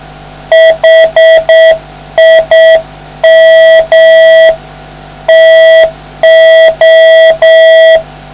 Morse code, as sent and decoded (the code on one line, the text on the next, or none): .... .. -- / - ---
HIM TO